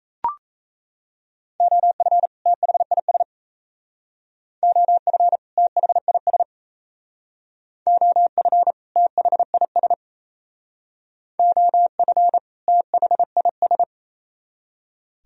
Morse code, heard as a keyed tone, about 36 wpm, 700 Hz. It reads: OF THIS OF THIS OF THIS OF THIS